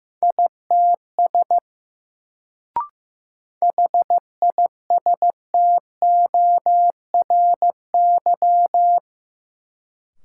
Morse code, heard as a keyed tone, about 15 wpm, 700 Hz. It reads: ITS HISTORY
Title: Sentences from Top 600 Words - New Word Drill - 15wpm